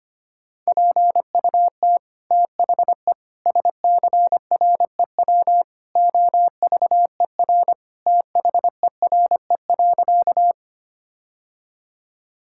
Morse code, heard as a keyed tone, about 25 wpm, 700 Hz. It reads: PUT THE SCREW OVER THERE.
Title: Sentences from Top 1600 Words 25wpm